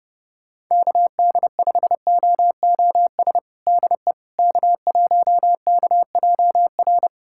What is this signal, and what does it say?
Morse code, heard as a keyed tone, about 30 wpm, 700 Hz. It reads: KD5OOS DE K1KJR
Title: USA Callsign DE Callsign 30wpm